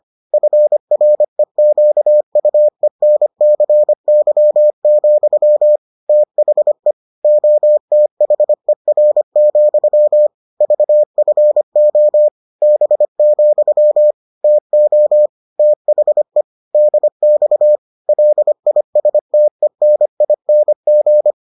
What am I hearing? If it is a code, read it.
Morse code, heard as a keyed tone, about 25 wpm, 600 Hz. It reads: FREQUENCY, THE OTHER, VFO B, TO THE DX LISTENING